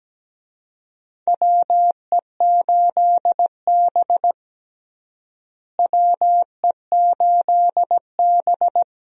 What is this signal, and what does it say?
Morse code, heard as a keyed tone, about 17 wpm, 700 Hz. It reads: WE8B WE8B
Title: USA Call Signs - 4 Characters - Repeated 2x - 17wpm